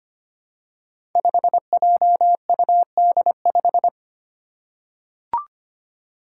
Morse code, heard as a keyed tone, about 25 wpm, 700 Hz. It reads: 5JUD5
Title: Five Challenging Letter-Number Combinations 25wpm